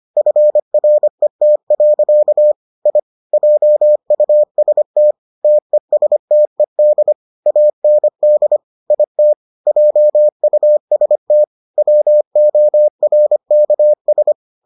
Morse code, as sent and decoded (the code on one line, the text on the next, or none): ..-. .-. - .-.-.- / .. / .--- ..- ... - / - . ... - . -.. / .- -. -.. / .. - / .--- ..- ... - / .-- --- .-. -.- ...
FRET. I JUST TESTED AND IT JUST WORKS